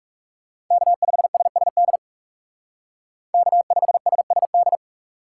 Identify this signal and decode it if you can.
Morse code, heard as a keyed tone, about 36 wpm, 700 Hz. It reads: K5SSD K5SSD
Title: Speed Racing - USA 1x3 Call Signs 30wpm